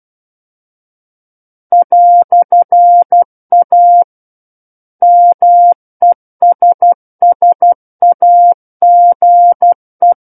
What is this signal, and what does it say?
Morse code, heard as a keyed tone, about 12 wpm, 700 Hz. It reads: "A MESSAGE
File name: Book One / 03-Chapter 3 - 12 WPM